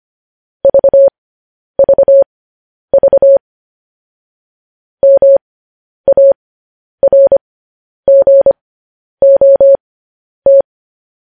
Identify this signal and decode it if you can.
Morse code, heard as a keyed tone, about 25 wpm, spaced as for 10 wpm, 550 Hz. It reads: VVV MARGOT
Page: news headlines in morse code at 10 wpm